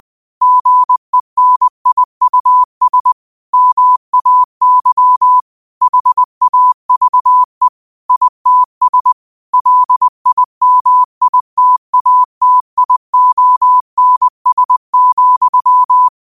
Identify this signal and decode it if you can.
Morse code, heard as a keyed tone, about 20 wpm, 1000 Hz. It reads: GENIUS MAY HAVE ITS LIMITATIONS,
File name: qotdfiles / 20wpm